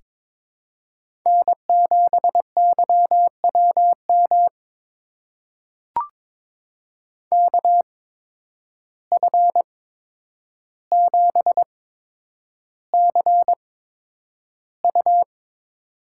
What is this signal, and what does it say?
Morse code, heard as a keyed tone, about 21 wpm, spaced as for 9 wpm, 700 Hz. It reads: N7YWM KF7CU
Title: USA Call Signs - 5 Characters - 8x Character Spacing - 22wpm